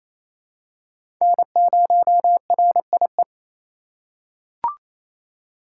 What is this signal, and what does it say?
Morse code, heard as a keyed tone, about 28 wpm, 700 Hz. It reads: N0RIE
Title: USA Call Signs - 4x Character Spacing - 28wpm